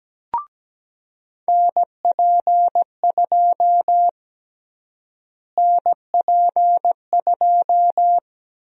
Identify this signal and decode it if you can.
Morse code, heard as a keyed tone, about 17 wpm, 700 Hz. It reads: NP2 NP2
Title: USA Call Sign Prefixes - Repeated 2x - 17wpm